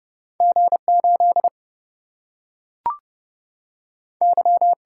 Morse code, heard as a keyed tone, about 30 wpm, 700 Hz. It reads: G 8 Y
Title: Single-Letter Single-Number 30wpm